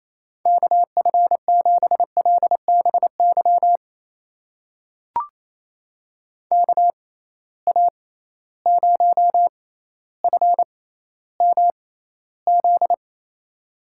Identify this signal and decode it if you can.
Morse code, heard as a keyed tone, about 28 wpm, spaced as for 14 wpm, 700 Hz. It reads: KF7LBY KA0FMZ
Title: USA Call Signs - 6 Characters - 6x Character Spacing - 28wpm